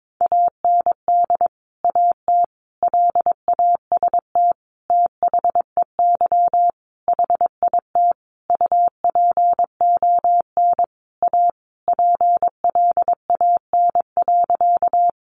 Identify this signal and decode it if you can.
Morse code, heard as a keyed tone, about 22 wpm, 700 Hz. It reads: AND AT LAST THEY HIT UPON A PLAN.
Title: Sentences from Top 1200 Words 22wpm